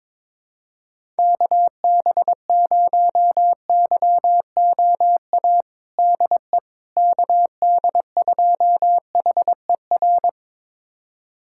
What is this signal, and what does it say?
Morse code, heard as a keyed tone, about 22 wpm, 700 Hz. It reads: KB0YOA DE KD2HER
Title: USA Callsign DE Callsign - 6 Characters - 22wpm